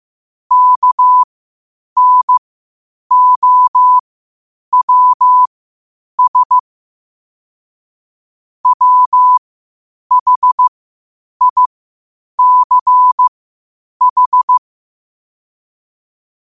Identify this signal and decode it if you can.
Morse code, heard as a keyed tone, about 15 wpm, spaced as for 8 wpm, 1000 Hz. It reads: KNOWS WHICH
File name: qotdfiles / 8wpm